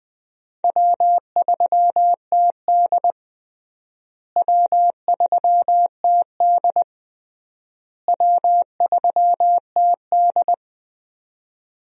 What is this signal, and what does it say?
Morse code, heard as a keyed tone, about 21 wpm, 700 Hz. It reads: W3TD W3TD W3TD
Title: USA Call Signs - 4 Characters - Repeated 3x - 20wpm